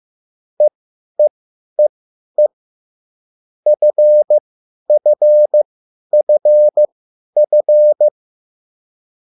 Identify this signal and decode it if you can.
Morse code, heard as a keyed tone, about 15 wpm, spaced as for 10 wpm, 600 Hz. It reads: EEEE FFFF
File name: QSL / CW lesson